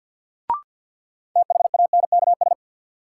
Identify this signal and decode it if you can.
Morse code, heard as a keyed tone, about 47 wpm, 700 Hz. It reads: THANKS
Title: Sentences from Top 600 Words - New Word Drill - 50wpm